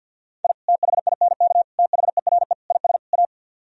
Morse code, heard as a keyed tone, about 49 wpm, 700 Hz. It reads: I THINK THERE IS A
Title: Speed-Racing - Sets of 5 Words 50wpm